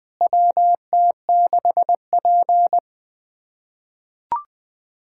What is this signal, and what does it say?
Morse code, heard as a keyed tone, about 21 wpm, 700 Hz. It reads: WT6P
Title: K1USN SST QSO Practice - 20wpm